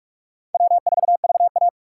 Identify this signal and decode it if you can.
Morse code, heard as a keyed tone, about 46 wpm, 700 Hz. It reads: W4VA
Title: USA Call Signs - 4 Characters - 2x Character Spacing - 45wpm